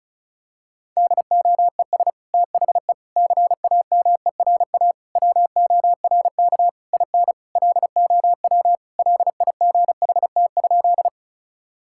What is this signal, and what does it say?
Morse code, heard as a keyed tone, about 36 wpm, 700 Hz. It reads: DOES THE CAMERA WORK IN LOW LIGHT?
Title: Sentences from Top 1000 Words - Review Part 5 - 35wpm